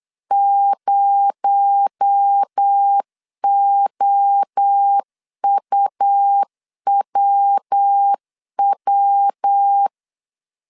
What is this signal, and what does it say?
Morse code, heard as a keyed tone, about 8 wpm, 800 Hz. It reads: 0OUWW
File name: 8wpm